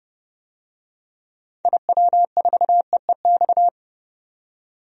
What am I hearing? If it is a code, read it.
Morse code, heard as a keyed tone, about 29 wpm, 700 Hz. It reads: IW4EEX
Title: DXCC-100 Call Signs - 6 Characters - 30wpm